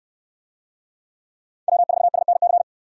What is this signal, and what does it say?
Morse code, heard as a keyed tone, about 64 wpm, 700 Hz. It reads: W4SNC